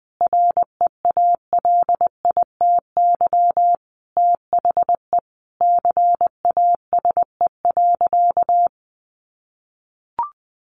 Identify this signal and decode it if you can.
Morse code, heard as a keyed tone, about 20 wpm, 700 Hz. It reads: REALITY THE CASE.
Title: Sentences from Top 1200 Words 20wpm